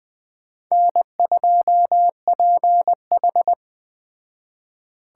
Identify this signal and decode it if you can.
Morse code, heard as a keyed tone, about 20 wpm, 700 Hz. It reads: N2PH
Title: Speed Racing - USA 1x2 Call Signs 17wpm